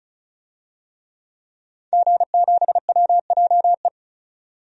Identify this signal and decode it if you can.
Morse code, heard as a keyed tone, about 34 wpm, 700 Hz. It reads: G7WJE
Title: DXCC-100 Call Signs - 5 Characters - 35wpm